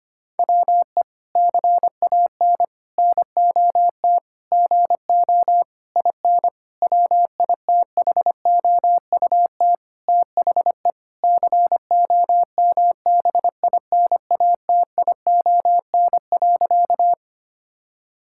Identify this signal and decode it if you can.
Morse code, heard as a keyed tone, about 25 wpm, 700 Hz. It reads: WE CAN NOT GO IN WITHOUT THE COMBINATION.
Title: Sentences from Top 1000 Words - Review Part 6 - 25wpm